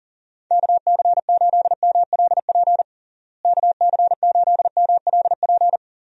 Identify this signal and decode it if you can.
Morse code, heard as a keyed tone, about 40 wpm, 700 Hz. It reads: KC8MLP KC8MLP
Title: USA Call Signs - 6 Characters - Repeated 3x - 40wpm